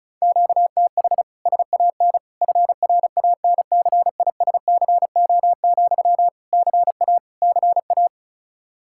Morse code, heard as a keyed tone, about 34 wpm, 700 Hz. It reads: QTH SAN FRANCISCO, CA CA
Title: K1USN SST QSO Practice 35wpm